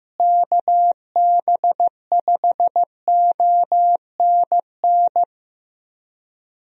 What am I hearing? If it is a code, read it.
Morse code, heard as a keyed tone, about 15 wpm, 700 Hz. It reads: KB5ONN
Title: USA Call Signs - 6 Characters - Repeated 3x - 15wpm